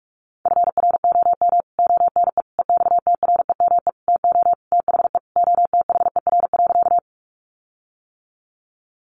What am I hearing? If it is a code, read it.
Morse code, heard as a keyed tone, about 44 wpm, 700 Hz. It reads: FROM ONE EXTREME TO THE OTHER.
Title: Sentences from Top 1300 Words 45wpm